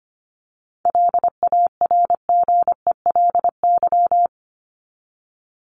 Morse code, heard as a keyed tone, about 25 wpm, 700 Hz. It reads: LARGELY
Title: LY Suffix - Top 100 Words with Base Words 25wpm